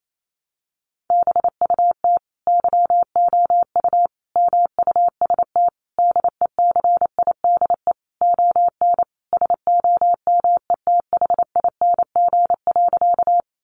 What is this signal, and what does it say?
Morse code, heard as a keyed tone, about 28 wpm, 700 Hz. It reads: BUT YOU MUST DECIDE ON SOMETHING.
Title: Sentences from Top 600 Words 28wpm